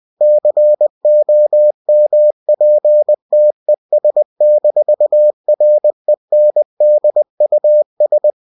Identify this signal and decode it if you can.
Morse code, heard as a keyed tone, about 20 wpm, 600 Hz. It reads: COMPTES-RENDUS